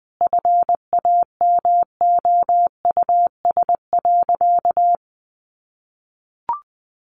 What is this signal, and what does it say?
Morse code, heard as a keyed tone, about 20 wpm, 700 Hz. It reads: FAMOUS.
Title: Sentences from Top 1500 Words 20wpm